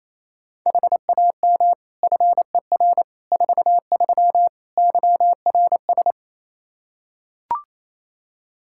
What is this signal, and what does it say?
Morse code, heard as a keyed tone, about 28 wpm, 700 Hz. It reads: HAM FER 43 YRS
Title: K1USN SST QSO Practice 28wpm